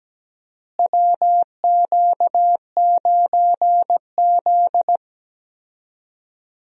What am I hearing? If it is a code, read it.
Morse code, heard as a keyed tone, about 17 wpm, 700 Hz. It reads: WQ9Z
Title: USA Call Signs - 4 Characters - Repeated 3x - 17wpm